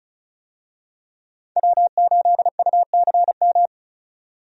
Morse code, heard as a keyed tone, about 34 wpm, 700 Hz. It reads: W8UCM
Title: USA Call Signs - 5 Characters - 35wpm